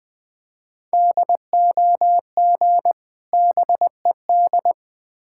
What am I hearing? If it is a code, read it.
Morse code, heard as a keyed tone, about 20 wpm, 700 Hz. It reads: DOG BED
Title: Difficult Words with Hints - 20wpm